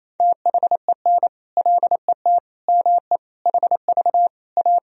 Morse code, heard as a keyed tone, about 28 wpm, 700 Hz. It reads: THEN LET ME HV A
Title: Sentences from Top 300 Words and Abbreviations 28wpm